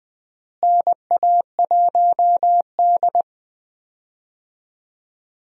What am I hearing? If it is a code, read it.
Morse code, heard as a keyed tone, about 20 wpm, 700 Hz. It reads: NA1D
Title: USA Call Signs - 4 Characters - Repeated 3x - 20wpm